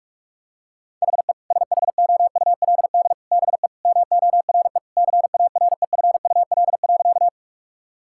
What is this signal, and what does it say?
Morse code, heard as a keyed tone, about 47 wpm, 700 Hz. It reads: HE SHOULD BE MORE CAREFUL.
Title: Sentences from Top 1600 Words 45wpm